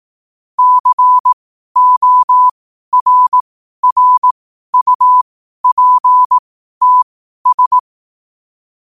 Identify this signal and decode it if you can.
Morse code, heard as a keyed tone, about 18 wpm, spaced as for 12 wpm, 1000 Hz. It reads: CORRUPTS
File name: qotdfiles / 12wpm